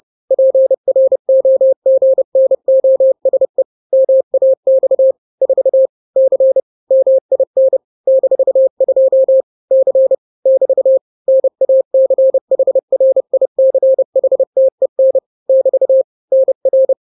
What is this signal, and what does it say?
Morse code, heard as a keyed tone, about 30 wpm, 530 Hz. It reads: PROGNOSE MAX 4 C MIN -2 C <BT> NACHRICHTEN <BT> NR
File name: daily / 29wpm